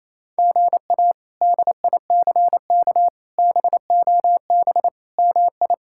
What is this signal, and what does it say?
Morse code, heard as a keyed tone, about 28 wpm, 700 Hz. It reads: GA DICK BOB MI
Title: K1USN SST QSO Practice - 28wpm